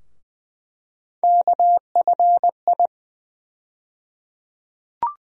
Morse code, heard as a keyed tone, about 20 wpm, 700 Hz. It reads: KFI
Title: USA Call Sign Postfixes - 3 Letters 20wpm